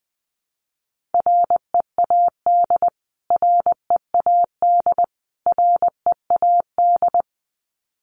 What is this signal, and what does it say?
Morse code, heard as a keyed tone, about 20 wpm, 700 Hz. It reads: READ READ READ
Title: Top 300 Words Repeated 20wpm